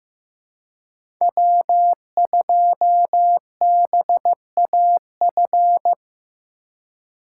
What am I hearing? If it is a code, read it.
Morse code, heard as a keyed tone, about 15 wpm, 700 Hz. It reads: W2BAF
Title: USA Call Signs - 2x Character Spacing - 15wpm